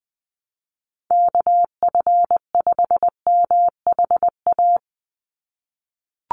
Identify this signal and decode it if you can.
Morse code, heard as a keyed tone, about 20 wpm, 700 Hz. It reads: KF5MHA
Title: USA Call Signs 20wpm